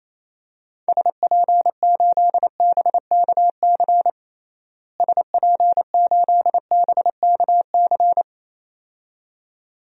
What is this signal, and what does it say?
Morse code, heard as a keyed tone, about 27 wpm, 700 Hz. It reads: SP8BKC SP8BKC
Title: DXCC-100 Call Signs - 6 Characters - Repeated 2x - 28wpm